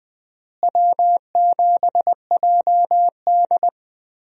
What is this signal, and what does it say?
Morse code, heard as a keyed tone, about 20 wpm, 700 Hz. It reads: W7JD
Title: USA Call Signs - 4 Characters - Repeated 3x - 20wpm